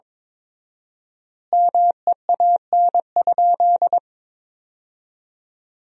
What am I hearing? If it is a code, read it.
Morse code, heard as a keyed tone, about 22 wpm, 700 Hz. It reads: MEAN?
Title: 4x Word-Spacing - Sentences from Top 200 Words - 22wpm